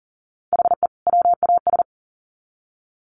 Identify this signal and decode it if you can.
Morse code, heard as a keyed tone, about 40 wpm, 700 Hz. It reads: HE WAS
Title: Sets of 2 Words 40wpm